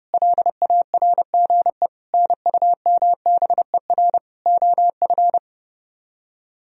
Morse code, heard as a keyed tone, about 34 wpm, 690 Hz. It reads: LARGE NUMBER OF
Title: Speed-Racing - Sets of 4 Words 30wpm